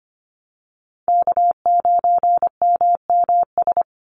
Morse code, heard as a keyed tone, about 25 wpm, 700 Hz. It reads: K9MMS
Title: Call Signs Repeated 2x - 25wpm